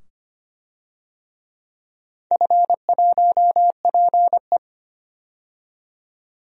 Morse code, heard as a keyed tone, about 25 wpm, 700 Hz. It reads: F1PE
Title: DXCC-100 Call Signs - 4 Characters - 25wpm